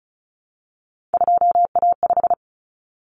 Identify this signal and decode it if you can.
Morse code, heard as a keyed tone, about 34 wpm, 700 Hz. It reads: 2A5
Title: Common Cut Numbers - 3-Digit 35wpm